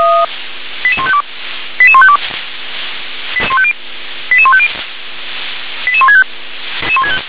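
Five-tone selective call (ZVEI), heard as quick runs of five tones, tones of about 70 ms. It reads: 88152 88152 88150 88165
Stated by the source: ZVEI five-tone call received by SDR.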